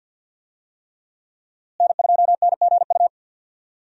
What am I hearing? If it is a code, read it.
Morse code, heard as a keyed tone, about 51 wpm, 700 Hz. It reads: N2NGU
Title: USA Call Signs - 5 Characters - Repeated 3x - 50wpm